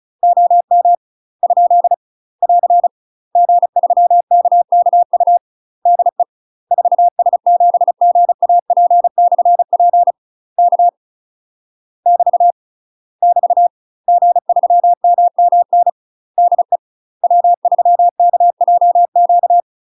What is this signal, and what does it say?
Morse code, heard as a keyed tone, about 34 wpm, 700 Hz. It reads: OM ? <AR> G3KKU DE 4S7GAP/P K <BT> <BT> G3MMN DE W3KJQ
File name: UK MASTER QSOs TRAINING / 35WPM / UK MASTER QSOs 0030